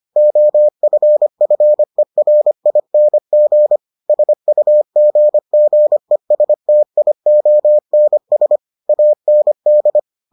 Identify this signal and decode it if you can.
Morse code, heard as a keyed tone, about 25 wpm, 600 Hz. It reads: OFFERING SUGGESTIONS AND